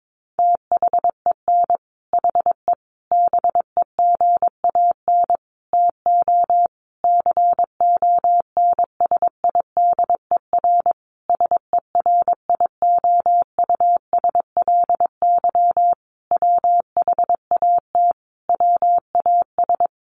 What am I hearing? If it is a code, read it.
Morse code, heard as a keyed tone, about 22 wpm, 700 Hz. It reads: THEN HE BEGAN TO CONSIDER SERIOUSLY WHAT WAS
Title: Sentences from Top 1200 Words 22wpm